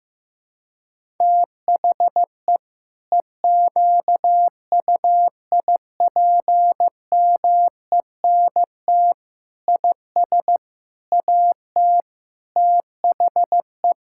Morse code, heard as a keyed tone, about 15 wpm, 700 Hz. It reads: THE EQUIPMENT IS AT THE
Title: Sentences from Top 1000 Words - Review Part 5 - 15wpm